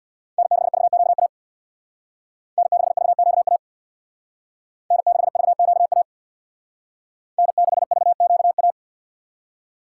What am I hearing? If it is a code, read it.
Morse code, heard as a keyed tone, about 61 wpm, 700 Hz. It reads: N6VQA N6VQA N6VQA N6VQA
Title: Speed Racing - USA 1x3 Call Signs 50wpm